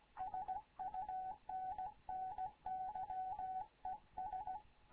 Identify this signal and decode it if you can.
Morse code, heard as a keyed tone, about 16 wpm, 750 Hz. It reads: SUNNY ES
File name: Morze-HA2MN / hangp / 2 11 GYAKORLAT